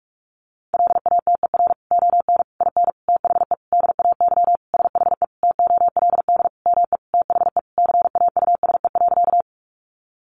Morse code, heard as a keyed tone, about 44 wpm, 700 Hz. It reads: LATER ON IN THE DAY SHE TOLD ME THE CAUSE.
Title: Sentences from Top 700 Words 45wpm